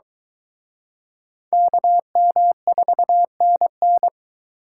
Morse code, heard as a keyed tone, about 24 wpm, 700 Hz. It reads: KM4NN